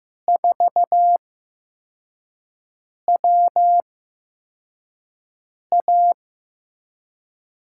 Morse code, heard as a keyed tone, about 15 wpm, spaced as for 4 wpm, 700 Hz. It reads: 4WA
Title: USA Call Signs - 5 Characters - 8x Character Spacing - 15wpm